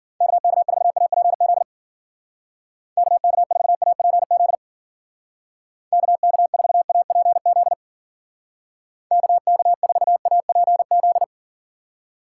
Morse code, heard as a keyed tone, about 52 wpm, 700 Hz. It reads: KK4APZ KK4APZ KK4APZ KK4APZ